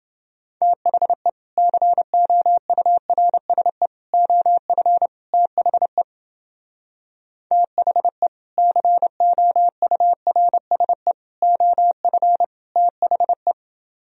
Morse code, heard as a keyed tone, about 29 wpm, 700 Hz. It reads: THE COURSE OF THE THE COURSE OF THE